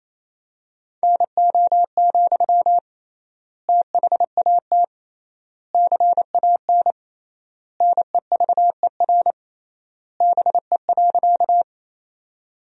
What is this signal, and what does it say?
Morse code, heard as a keyed tone, about 28 wpm, 700 Hz. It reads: NO, THAT CAN NEVER BE.
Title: 3x Word-Spacing - Sentences from Top 300 Words - 28wpm